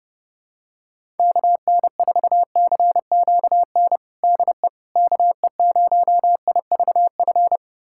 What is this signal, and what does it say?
Morse code, heard as a keyed tone, about 30 wpm, 700 Hz. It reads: KN4CQN DE KE0IVF
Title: USA Callsign DE Callsign - 6 Characters - 30wpm